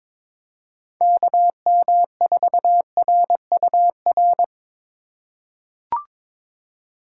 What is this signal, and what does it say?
Morse code, heard as a keyed tone, about 22 wpm, 700 Hz. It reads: KM4RUR
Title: USA Call Signs - 6 Characters - 2x Character Spacing - 22wpm